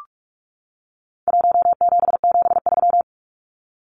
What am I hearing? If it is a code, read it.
Morse code, heard as a keyed tone, about 44 wpm, 700 Hz. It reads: 1773
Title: Numbers - Contest Style - 4-Digit 45wpm